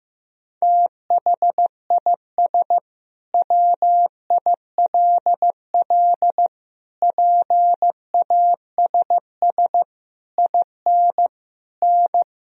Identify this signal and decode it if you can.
Morse code, heard as a keyed tone, about 15 wpm, 700 Hz. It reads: THIS WILL PASS IN N
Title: Sentences from Top 400 Words and Abbreviations 15wpm